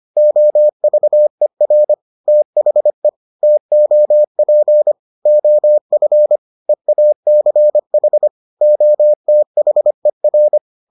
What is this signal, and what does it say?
Morse code, heard as a keyed tone, about 25 wpm, 600 Hz. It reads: OVER THE TOP OF EACH OTHER